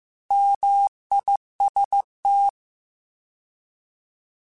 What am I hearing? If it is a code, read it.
Morse code, heard as a keyed tone, about 15 wpm, 800 Hz. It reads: MIST